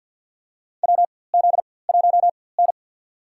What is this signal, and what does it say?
Morse code, heard as a keyed tone, about 51 wpm, spaced as for 23 wpm, 700 Hz. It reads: WZ1N
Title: USA Call Signs - 4x Character Spacing - 50wpm